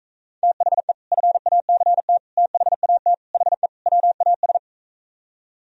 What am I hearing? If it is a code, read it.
Morse code, heard as a keyed tone, about 45 wpm, 700 Hz. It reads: THE FACT THAT HE WAS